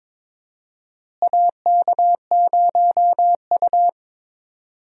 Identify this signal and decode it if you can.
Morse code, heard as a keyed tone, about 22 wpm, 700 Hz. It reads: AK0U
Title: USA Call Signs - 4 Characters - Repeated 3x - 22wpm